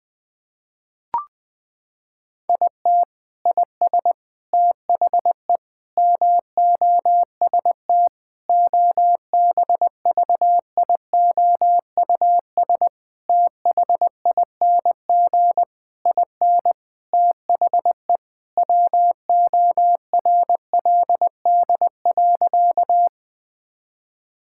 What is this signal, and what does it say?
Morse code, heard as a keyed tone, about 20 wpm, 700 Hz. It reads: IT IS THE MOST OBVIOUS THING IN THE WORLD.
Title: Sentences from Top 1600 Words 20wpm